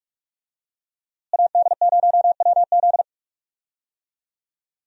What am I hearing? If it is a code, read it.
Morse code, heard as a keyed tone, about 42 wpm, 700 Hz. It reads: AD0WZ
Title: USA Call Signs - 5 Characters - Repeated 2x - 45wpm